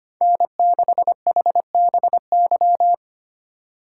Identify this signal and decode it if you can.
Morse code, heard as a keyed tone, about 25 wpm, 700 Hz. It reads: N6HBY
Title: USA Call Signs - 4x Character Spacing - 25wpm